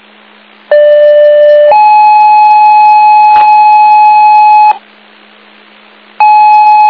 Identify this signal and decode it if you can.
Two-tone page, heard as a one-second tone, then a three-second tone, about 590 then 850 Hz. It is heard from 0.7 to 4.7 s.